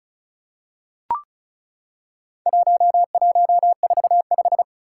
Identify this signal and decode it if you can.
Morse code, heard as a keyed tone, about 36 wpm, 700 Hz. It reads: 1145
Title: Four Challenging Letter-Number Combinations 35wpm